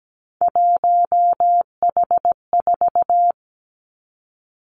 Morse code, heard as a keyed tone, about 17 wpm, 700 Hz. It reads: 1H4
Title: Three Challenging Letter-Number Combinations 17wpm